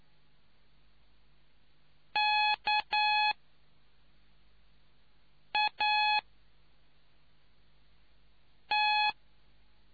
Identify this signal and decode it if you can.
Morse code, heard as a keyed tone, about 9 wpm, 840 Hz. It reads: K A T